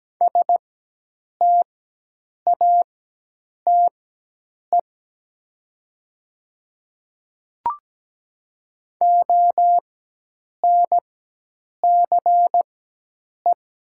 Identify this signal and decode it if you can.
Morse code, heard as a keyed tone, about 17 wpm, spaced as for 8 wpm, 700 Hz. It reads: STATE ONCE
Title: 4x Character Spacing - Top 500 Words - 17wpm